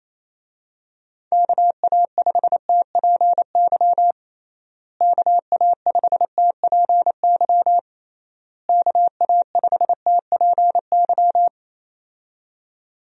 Morse code, heard as a keyed tone, about 28 wpm, 700 Hz. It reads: KA5TPY KA5TPY KA5TPY
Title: USA Call Signs - 6 Characters - Repeated 3x - 28wpm